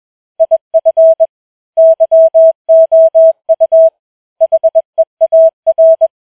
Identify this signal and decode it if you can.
Morse code, heard as a keyed tone, about 21 wpm, 650 Hz. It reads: IF YOU HEAR
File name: challenge-185